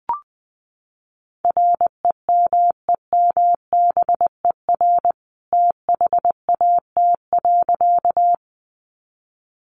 Morse code, heard as a keyed tone, about 20 wpm, 700 Hz. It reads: REMEMBER THAT.
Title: A Sound Of Thunder 20wpm